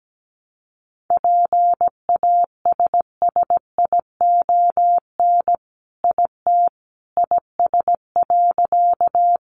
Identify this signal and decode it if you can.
Morse code, heard as a keyed tone, about 17 wpm, 700 Hz. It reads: PASSION IT IS.
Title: Sentences from Top 1300 Words 17wpm